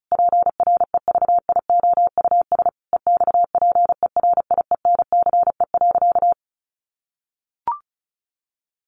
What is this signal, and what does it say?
Morse code, heard as a keyed tone, about 34 wpm, 700 Hz. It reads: PREVIOUS EXPERIENCE.
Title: Sentences from Top 1500 Words 35wpm